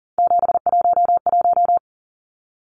Morse code, heard as a keyed tone, about 40 wpm, 700 Hz. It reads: 711
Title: Numbers - 3-Digit 40wpm